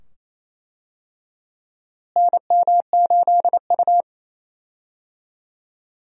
Morse code, heard as a keyed tone, about 28 wpm, 700 Hz. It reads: NM8U
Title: USA Call Signs - 4 Characters - Repeated 2x - 28wpm